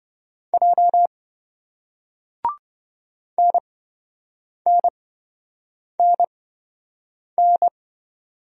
Morse code, heard as a keyed tone, about 28 wpm, spaced as for 7 wpm, 700 Hz. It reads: J NNNN